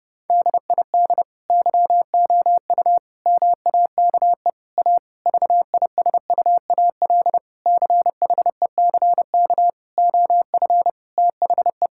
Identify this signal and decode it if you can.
Morse code, heard as a keyed tone, about 28 wpm, 700 Hz. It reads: DID YOU MAKE A VISUAL CHECK OF THE
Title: Sentences from Top 1000 Words - Review Part 6 - 30wpm